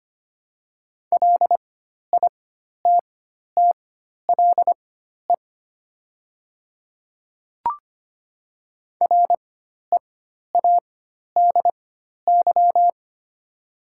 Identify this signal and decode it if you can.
Morse code, heard as a keyed tone, about 25 wpm, spaced as for 12 wpm, 700 Hz. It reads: LITTLE READY
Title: 4x Character Spacing - Top 500 Words - 25wpm